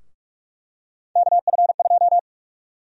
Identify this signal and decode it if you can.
Morse code, heard as a keyed tone, about 46 wpm, 700 Hz. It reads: KF2
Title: USA Call Sign Prefixes - 45wpm